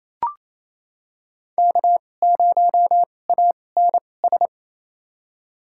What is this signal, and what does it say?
Morse code, heard as a keyed tone, about 28 wpm, spaced as for 20 wpm, 700 Hz. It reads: K0ANS
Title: USA Call Signs - 5 Characters - 2x Character Spacing - 28wpm